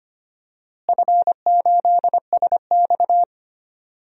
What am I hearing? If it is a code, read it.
Morse code, heard as a keyed tone, about 25 wpm, 700 Hz. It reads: F8SX
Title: DXCC-100 Call Signs - 4 Characters - Repeated 2x - 25wpm